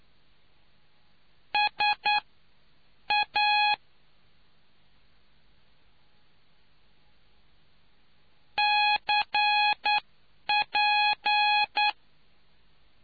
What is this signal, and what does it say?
Morse code, heard as a keyed tone, about 9 wpm, spaced as for 7 wpm, 840 Hz. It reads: SA CP